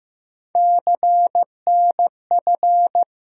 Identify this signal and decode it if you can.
Morse code, heard as a keyed tone, about 15 wpm, 700 Hz. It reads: CNF
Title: USA Call Sign Postfixes - 3 Letters 15wpm